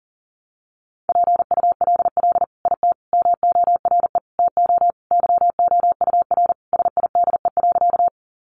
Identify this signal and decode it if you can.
Morse code, heard as a keyed tone, about 40 wpm, 700 Hz. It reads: PULL IT MORE TO YOUR SIDE.
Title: Sentences from Top 400 Words 40wpm